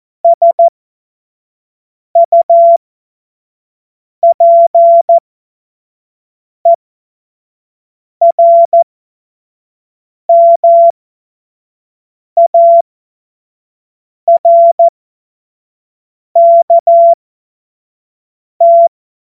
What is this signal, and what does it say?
Morse code, heard as a keyed tone, about 14 wpm, spaced as for 5 wpm, 680 Hz. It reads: SUPERMARKT
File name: Morsekurs / Witze / Witz 17. August 2021, 25 F70